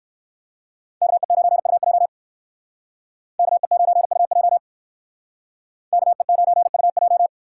Morse code, heard as a keyed tone, about 60 wpm, 700 Hz. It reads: KE9UJ KE9UJ KE9UJ